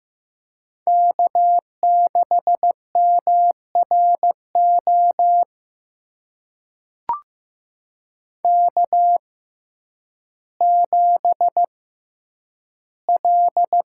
Morse code, heard as a keyed tone, about 15 wpm, spaced as for 9 wpm, 700 Hz. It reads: K6MRO K7L
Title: USA Call Signs - 5 Characters - 6x Character Spacing - 15wpm